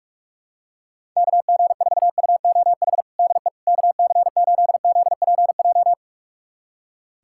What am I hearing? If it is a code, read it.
Morse code, heard as a keyed tone, about 42 wpm, 700 Hz. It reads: KG4UOH DE KC8ZPJ
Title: USA Callsign DE Callsign - 6 Characters - 45wpm